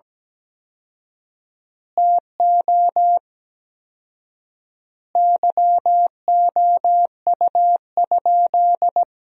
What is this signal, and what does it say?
Morse code, heard as a keyed tone, about 17 wpm, 700 Hz. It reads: TO YOU?
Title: 4x Word-Spacing - Sentences from Top 100 Words - 17wpm